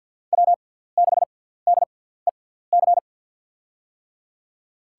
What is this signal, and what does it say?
Morse code, heard as a keyed tone, about 54 wpm, spaced as for 17 wpm, 700 Hz. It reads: W6DEC
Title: USA Call Signs - 5 Characters - 6x Character Spacing - 50wpm